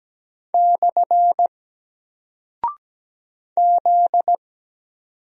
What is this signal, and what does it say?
Morse code, heard as a keyed tone, about 17 wpm, 700 Hz. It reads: / Z
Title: Single Character (Letter:Number:Punctuation:Prosign) 17wpm